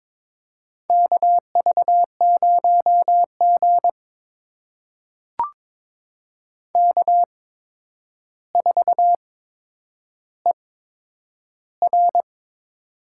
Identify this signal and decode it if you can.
Morse code, heard as a keyed tone, about 21 wpm, spaced as for 9 wpm, 700 Hz. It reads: KV0G K4ER
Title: USA Call Signs - 4 Characters - 8x Character Spacing - 22wpm